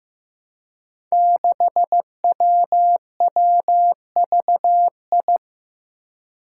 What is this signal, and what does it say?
Morse code, heard as a keyed tone, about 15 wpm, 700 Hz. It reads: 6WWVI
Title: Five Challenging Letter-Number Combinations 15wpm